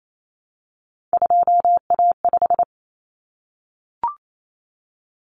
Morse code, heard as a keyed tone, about 28 wpm, 700 Hz. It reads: 2A5
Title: Common Cut Numbers - 3-Digit 28wpm